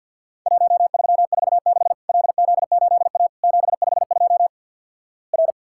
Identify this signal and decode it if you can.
Morse code, heard as a keyed tone, about 50 wpm, 700 Hz. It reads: 1346 LZ8A 752 R
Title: Worked All Europe DX Contest - QTC Exchange - 50wpm